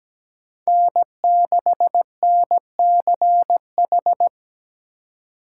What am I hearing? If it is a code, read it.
Morse code, heard as a keyed tone, about 17 wpm, 700 Hz. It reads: N6NCH
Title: USA Call Signs - 5 Characters - 8x Character Spacing - 17wpm